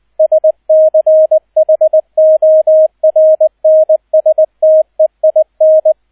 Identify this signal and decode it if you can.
Morse code, heard as a keyed tone, about 20 wpm, 630 Hz, with radio noise. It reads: SCHORNSTEIN
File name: 002 Zweiter Streich 20wpm